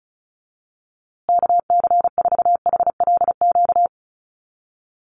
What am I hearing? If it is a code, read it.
Morse code, heard as a keyed tone, about 34 wpm, 700 Hz. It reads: KC4HLQ
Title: USA Call Signs Repeated 3x 35wpm